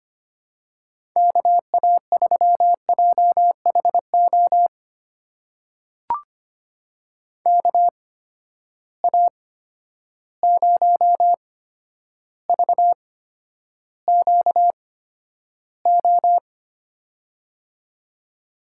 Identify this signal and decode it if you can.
Morse code, heard as a keyed tone, about 25 wpm, spaced as for 11 wpm, 700 Hz. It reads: KA3JHO KA0VQO
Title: USA Call Signs - 6 Characters - 8x Character Spacing - 25wpm